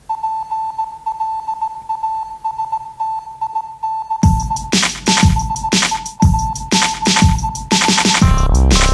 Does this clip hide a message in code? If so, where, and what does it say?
Yes: Morse code from 0.1 to 8.2 s, heard as a keyed tone, about 17 wpm, 880 Hz. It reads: PLASTIC BEACH